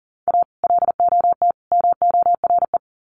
Morse code, heard as a keyed tone, about 40 wpm, 700 Hz. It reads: A LOT MORE
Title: Sets of 3 Words 40wpm